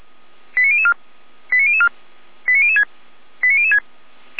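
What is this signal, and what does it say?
Five-tone selective call (ZVEI), heard as quick runs of five tones, tones of about 70 ms. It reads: 89004 89004 89006 89006
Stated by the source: ZVEI five-tone call 89017 received by SDR.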